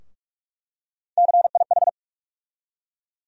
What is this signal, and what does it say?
Morse code, heard as a keyed tone, about 44 wpm, 700 Hz. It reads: CIH
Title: USA Call Sign Postfixes - 3 Letters 45wpm